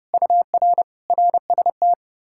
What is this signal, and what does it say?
Morse code, heard as a keyed tone, about 40 wpm, 700 Hz. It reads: UR RST